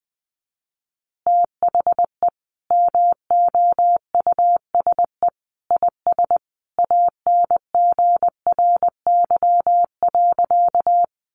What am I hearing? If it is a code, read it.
Morse code, heard as a keyed tone, about 20 wpm, 700 Hz. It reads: THE MOUSE IS ANGRY.
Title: Easy Phrases 20wpm